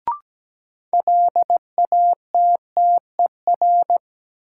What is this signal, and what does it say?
Morse code, heard as a keyed tone, about 17 wpm, 700 Hz. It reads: LATTER
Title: Sentences from Top 1600 Words - New Word Drill - 17wpm